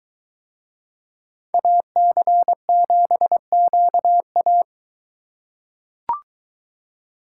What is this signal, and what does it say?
Morse code, heard as a keyed tone, about 21 wpm, 700 Hz. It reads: AC7QA